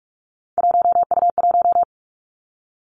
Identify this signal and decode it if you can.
Morse code, heard as a keyed tone, about 46 wpm, 700 Hz. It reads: 1U1
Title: Three Challenging Letter-Number Combinations 45wpm